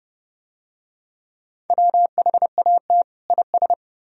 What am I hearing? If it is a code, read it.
Morse code, heard as a keyed tone, about 27 wpm, 700 Hz. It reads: WHAT IS